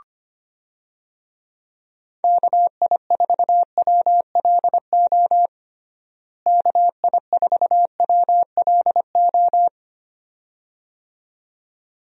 Morse code, heard as a keyed tone, about 25 wpm, 700 Hz. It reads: KI4WLO KI4WLO
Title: USA Call Signs - 6 Characters - Repeated 2x - 25wpm